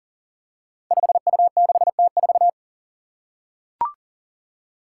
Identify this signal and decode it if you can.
Morse code, heard as a keyed tone, about 42 wpm, 700 Hz. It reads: 5U6T4
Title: Five Challenging Letter-Number Combinations 40wpm